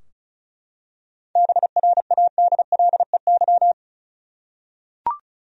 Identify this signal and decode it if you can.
Morse code, heard as a keyed tone, about 34 wpm, 700 Hz. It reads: BRADLEY
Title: USA Common Names - 35wpm